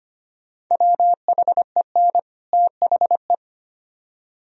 Morse code, heard as a keyed tone, about 34 wpm, spaced as for 30 wpm, 700 Hz. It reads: WHEN THE
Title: Speed-Racing - Sets of 2 Words 25wpm